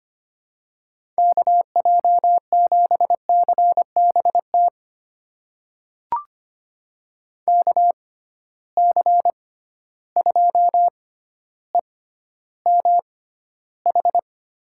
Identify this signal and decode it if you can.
Morse code, heard as a keyed tone, about 25 wpm, spaced as for 13 wpm, 700 Hz. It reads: KJ7CBT KC2EMH
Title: USA Call Signs - 6 Characters - 6x Character Spacing - 25wpm